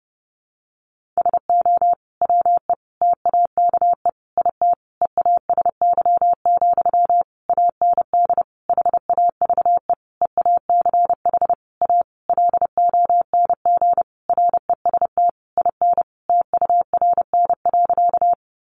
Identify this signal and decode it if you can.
Morse code, heard as a keyed tone, about 30 wpm, 700 Hz. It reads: SO WE TAKE IT EASY, AND HAVE EACH A LONG REST IN TURN.
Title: Sentences from Top 700 Words 30wpm